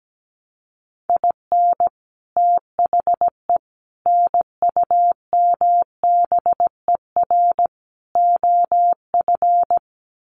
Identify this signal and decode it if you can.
Morse code, heard as a keyed tone, about 17 wpm, 700 Hz. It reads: IN THE NUMBER OF
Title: Sets of 4 Words 17wpm